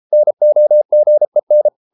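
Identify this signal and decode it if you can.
Morse code, heard as a keyed tone, about 33 wpm, 600 Hz. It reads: NOGEN
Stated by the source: Morse code at 33 wpm, QRQ.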